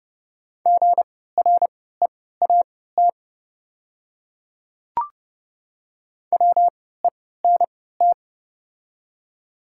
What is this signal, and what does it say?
Morse code, heard as a keyed tone, about 29 wpm, spaced as for 17 wpm, 700 Hz. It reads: GREAT WENT
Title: 3x Character Spacing - Top 200 Words - 30wpm